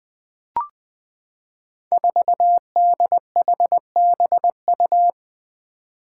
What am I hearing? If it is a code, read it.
Morse code, heard as a keyed tone, about 20 wpm, 700 Hz. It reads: 4DHBU
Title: Five Challenging Letter-Number Combinations 20wpm